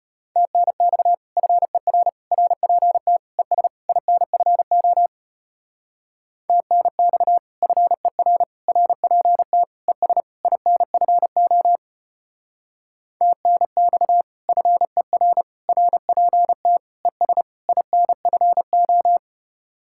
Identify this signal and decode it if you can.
Morse code, heard as a keyed tone, about 33 wpm, 700 Hz. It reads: TNX FER RPT ES INFO TNX FER RPT ES INFO TNX FER RPT ES INFO